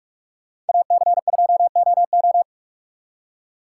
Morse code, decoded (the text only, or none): AC2OO